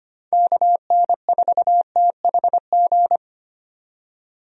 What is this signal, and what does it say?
Morse code, heard as a keyed tone, about 25 wpm, 700 Hz. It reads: KN4THG
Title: USA Call Signs - 6 Characters - 25wpm